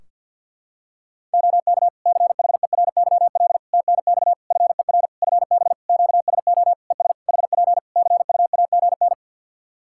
Morse code, heard as a keyed tone, about 50 wpm, 700 Hz. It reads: OK CHERYL TNX FER FB QSO ES HP CUAGN
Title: Practice QSO - Exchange 4 - Ending - 50wpm